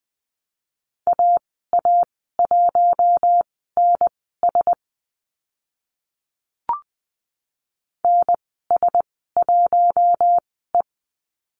Morse code, heard as a keyed tone, about 20 wpm, spaced as for 14 wpm, 700 Hz. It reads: AA1NS NS1E
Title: 042 Lesson 8g - Number 1 - Call Signs 20wpm